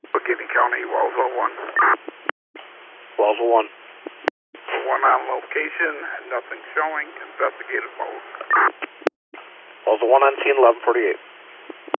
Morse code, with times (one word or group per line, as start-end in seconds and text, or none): none